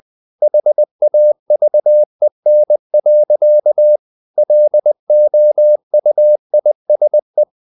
Morse code, heard as a keyed tone, about 20 wpm, 600 Hz. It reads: HAVEN. LOUISE
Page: CW med Gnister